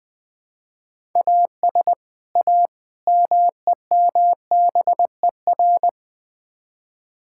Morse code, heard as a keyed tone, about 20 wpm, 700 Hz. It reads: AS A MEMBER
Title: Build A Phrase - 5-Word Phrases - 20wpm